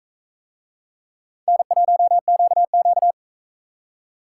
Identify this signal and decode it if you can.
Morse code, heard as a keyed tone, about 40 wpm, 700 Hz. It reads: N1QQ